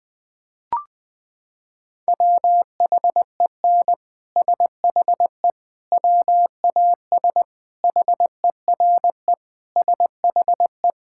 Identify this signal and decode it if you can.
Morse code, heard as a keyed tone, about 21 wpm, 700 Hz. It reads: WHEN SHE WAS HERE SHE
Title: Sentences with 3-Word Phrases - 20wpm